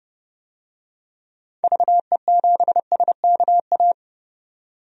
Morse code, heard as a keyed tone, about 30 wpm, 700 Hz. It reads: VE7SKA
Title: DXCC-100 Call Signs - 6 Characters - 30wpm